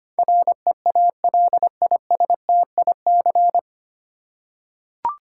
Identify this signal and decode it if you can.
Morse code, heard as a keyed tone, about 25 wpm, 700 Hz. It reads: REALISTIC
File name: Nine Letter Words - 25wpm